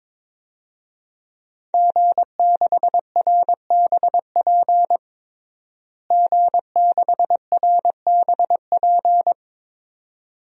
Morse code, heard as a keyed tone, about 22 wpm, 700 Hz. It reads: G6RBP G6RBP
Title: DXCC-100 Call Signs - 5 Characters - Repeated 2x - 22wpm